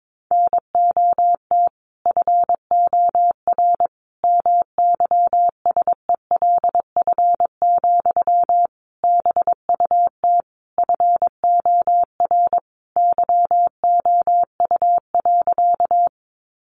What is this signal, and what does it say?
Morse code, heard as a keyed tone, about 22 wpm, 700 Hz. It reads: NOT FOR MYSELF, BUT FOR YOU.
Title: Sentences from Top 1300 Words 22wpm